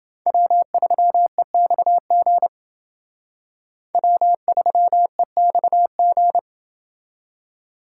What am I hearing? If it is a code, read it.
Morse code, heard as a keyed tone, about 28 wpm, 700 Hz. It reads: W3EXG W3EXG